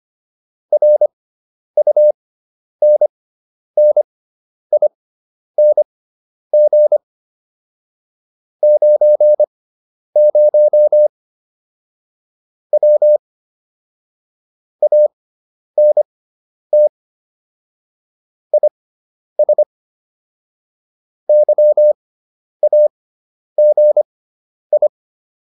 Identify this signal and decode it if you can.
Morse code, heard as a keyed tone, about 25 wpm, spaced as for 10 wpm, 600 Hz. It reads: RUNNING 90 W ANT IS YAGI